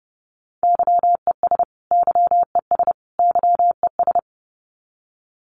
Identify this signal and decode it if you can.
Morse code, heard as a keyed tone, about 30 wpm, 700 Hz. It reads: YES YES YES
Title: Top 500 Words Repeated 30wpm